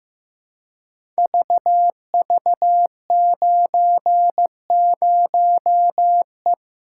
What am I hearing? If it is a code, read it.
Morse code, heard as a keyed tone, about 15 wpm, 700 Hz. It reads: VV90E
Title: Code Groups - 5 Letter:Numbers - 15wpm